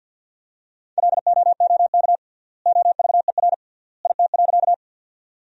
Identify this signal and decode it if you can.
Morse code, heard as a keyed tone, about 50 wpm, 700 Hz. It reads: LOOK OVER IT.
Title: 3x Word-Spacing - Sentences from Top 100 Words - 50wpm